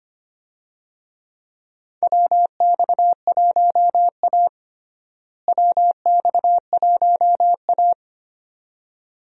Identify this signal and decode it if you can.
Morse code, heard as a keyed tone, about 25 wpm, 700 Hz. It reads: WX1A WX1A
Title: USA Call Signs - 4 Characters - Repeated 2x - 25wpm